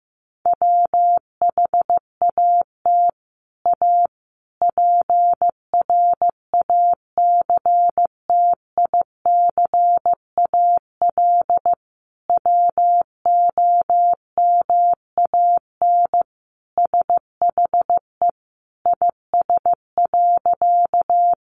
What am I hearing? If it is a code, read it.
Morse code, heard as a keyed tone, about 15 wpm, 700 Hz. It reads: WHAT A PRACTICAL WOMAN SHE IS.
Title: Sentences from Top 1400 Words 15wpm